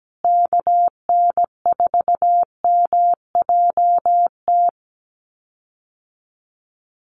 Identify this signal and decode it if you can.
Morse code, heard as a keyed tone, about 17 wpm, 700 Hz. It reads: KN4MJT
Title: USA Call Signs Repeated 3x 17wpm